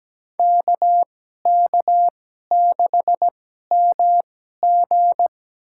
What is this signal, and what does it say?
Morse code, heard as a keyed tone, about 17 wpm, spaced as for 12 wpm, 700 Hz. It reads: KK6MG
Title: USA Call Signs - 5 Characters - 2x Character Spacing - 17wpm